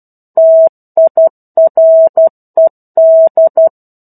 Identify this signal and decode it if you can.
Morse code, heard as a keyed tone, about 12 wpm, 650 Hz. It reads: TIRED
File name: 01-Chapter-01 12 WPM